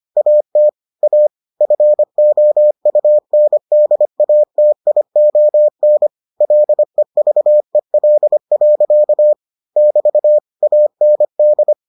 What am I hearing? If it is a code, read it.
Morse code, heard as a keyed tone, about 25 wpm, 600 Hz. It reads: AT A FOUNDATION LEVEL. <BT> AND